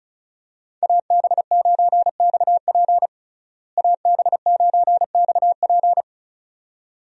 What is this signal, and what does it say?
Morse code, heard as a keyed tone, about 34 wpm, 700 Hz. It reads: AB9XP AB9XP
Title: USA Call Signs - 5 Characters - Repeated 2x - 35wpm